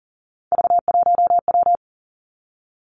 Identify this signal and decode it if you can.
Morse code, heard as a keyed tone, about 38 wpm, 700 Hz. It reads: V1W
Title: Three Challenging Letter-Number Combinations 40wpm